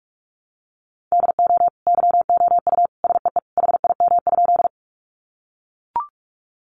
Morse code, heard as a keyed tone, about 44 wpm, 700 Hz. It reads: DO YOU SEE HIM?
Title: Sentences from Top 100 Words 45wpm